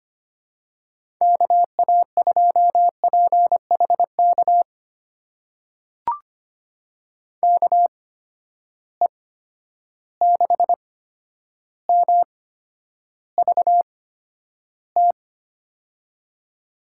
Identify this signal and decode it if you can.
Morse code, heard as a keyed tone, about 25 wpm, spaced as for 11 wpm, 700 Hz. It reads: KA2PHK KE6MVT
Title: USA Call Signs - 6 Characters - 8x Character Spacing - 25wpm